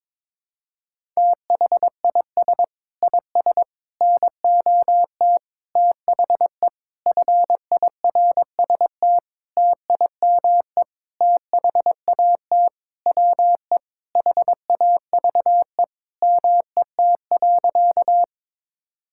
Morse code, heard as a keyed tone, about 22 wpm, 700 Hz. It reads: THIS IS NOT THE FIRST TIME THAT WE HAVE MET.
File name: Sentences with 7-Word Phrases - 22wpm